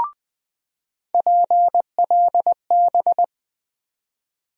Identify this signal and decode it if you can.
Morse code, heard as a keyed tone, about 21 wpm, 700 Hz. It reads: PLB
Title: USA Call Sign Postfixes - 3 Letters 20wpm